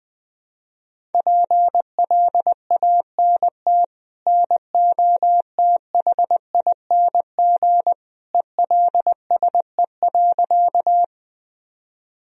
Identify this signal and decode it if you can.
Morse code, heard as a keyed tone, about 20 wpm, 700 Hz. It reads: PLANT NOTHING ELSE.
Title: Sentences from Top 1000 Words - Review Part 1 - 20wpm